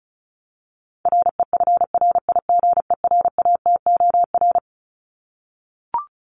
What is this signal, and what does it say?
Morse code, heard as a keyed tone, about 34 wpm, 700 Hz. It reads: REFRIGERATOR
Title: Top 1501-1750 Words 35wpm